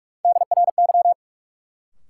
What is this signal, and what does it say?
Morse code, heard as a keyed tone, about 42 wpm, 700 Hz. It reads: DRY
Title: Sentences from Top 500 Words - New Word Drill - 45wpm